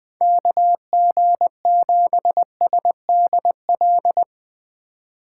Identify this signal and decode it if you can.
Morse code, heard as a keyed tone, about 20 wpm, 700 Hz. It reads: KG7SDL